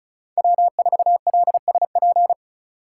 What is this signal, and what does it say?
Morse code, heard as a keyed tone, about 34 wpm, 700 Hz. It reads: W4LSP
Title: USA Call Signs - 5 Characters - 4x Character Spacing - 35wpm